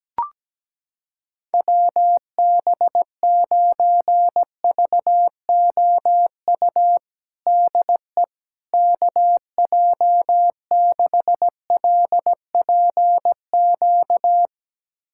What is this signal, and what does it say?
Morse code, heard as a keyed tone, about 17 wpm, 700 Hz. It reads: WB9VOU DE KJ6LPQ
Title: USA Callsign DE Callsign - 6 Characters - 17wpm